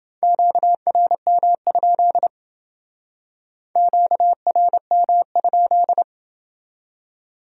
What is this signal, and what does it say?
Morse code, heard as a keyed tone, about 28 wpm, 700 Hz. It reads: QRM? QRM?